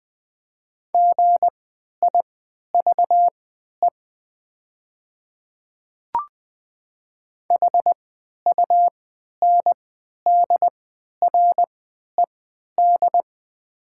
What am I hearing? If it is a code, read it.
Morse code, heard as a keyed tone, about 20 wpm, spaced as for 11 wpm, 700 Hz. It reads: GIVE HUNDRED
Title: 3x Character Spacing - Top 500 Words - 20wpm